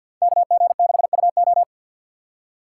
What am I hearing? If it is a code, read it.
Morse code, heard as a keyed tone, about 53 wpm, 700 Hz. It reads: KG6UO
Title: USA Call Signs - 5 Characters - 3x Character Spacing - 50wpm